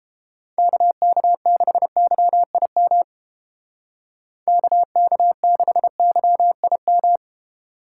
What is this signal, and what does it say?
Morse code, heard as a keyed tone, about 31 wpm, 700 Hz. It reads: KK6YIM KK6YIM